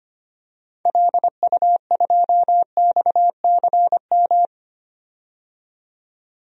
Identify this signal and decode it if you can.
Morse code, heard as a keyed tone, about 25 wpm, 690 Hz. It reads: LU2XCM
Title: DXCC-100 Call Signs - 6 Characters - 25wpm